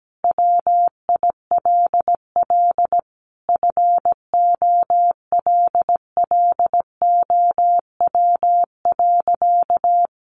Morse code, heard as a keyed tone, about 17 wpm, 700 Hz. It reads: WILL FOLLOW.
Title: Sentences from Top 1500 Words 17wpm